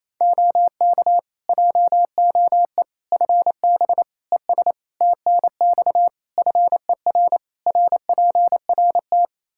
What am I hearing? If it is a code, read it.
Morse code, heard as a keyed tone, about 28 wpm, 700 Hz. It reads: OK JOE FB ES TNX FER RPRT
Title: K1USN SST QSO Practice 28wpm